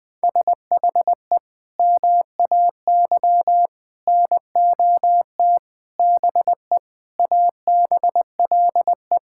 Morse code, heard as a keyed tone, about 20 wpm, 700 Hz. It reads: SHE MAY NOT BE ABLE
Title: Sentences with 5-Word Phrases - 20wpm